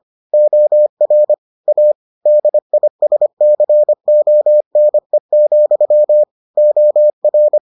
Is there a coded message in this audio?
Yes: Morse code from 0.3 to 7.6 s, heard as a keyed tone, about 25 wpm, 600 Hz. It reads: OR A DISCONE, OR